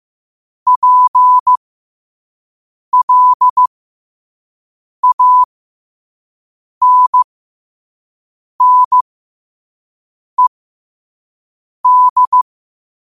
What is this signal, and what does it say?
Morse code, heard as a keyed tone, about 15 wpm, spaced as for 5 wpm, 1000 Hz. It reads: PLANNED
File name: qotdfiles / 5wpm